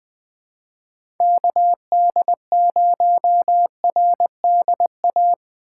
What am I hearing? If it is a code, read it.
Morse code, heard as a keyed tone, about 21 wpm, 700 Hz. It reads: KD0RDA
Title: USA Call Signs - 4x Character Spacing - 20wpm